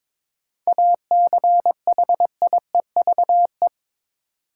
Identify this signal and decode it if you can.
Morse code, heard as a keyed tone, about 22 wpm, 700 Hz. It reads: ACHIEVE
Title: Sentences from Top 1000 Words - Review Part 5 - New Word Drill - 22wpm